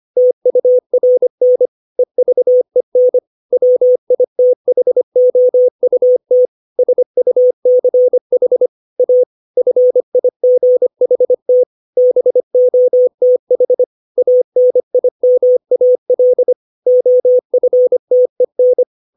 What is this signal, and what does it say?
Morse code, heard as a keyed tone, about 25 wpm, 500 Hz. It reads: TURN EVEN WITHOUT SUCH A FIGHT BOTH ANIMAL OFTEN
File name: 632 words / 632 Words 25WPM